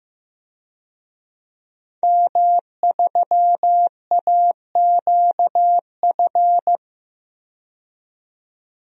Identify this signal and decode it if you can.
Morse code, heard as a keyed tone, about 15 wpm, 700 Hz. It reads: M3AQF
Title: DXCC-100 Call Signs - 5 Characters - 15wpm